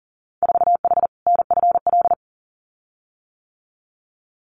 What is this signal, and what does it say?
Morse code, heard as a keyed tone, about 42 wpm, 700 Hz. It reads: 4H NFL
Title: Winter Field Day Exchanges Encore 40wpm